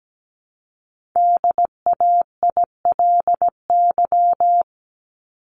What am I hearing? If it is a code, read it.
Morse code, heard as a keyed tone, about 17 wpm, 700 Hz. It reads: DAILY
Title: Top 1001-1250 Words 17wpm